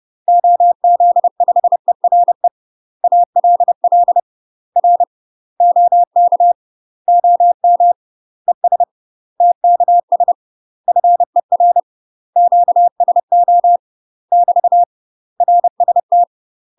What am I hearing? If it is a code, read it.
Morse code, heard as a keyed tone, about 30 wpm, 700 Hz. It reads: OZ5ERE ALL R OK OM ES TKS FER QSO <BT> RST
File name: UK MASTER QSOs TRAINING / 30WPM / UK MASTER QSOs 0049